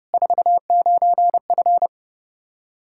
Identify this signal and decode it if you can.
Morse code, heard as a keyed tone, about 31 wpm, 700 Hz. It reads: 49F
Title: K1USN SST QSO Practice 30wpm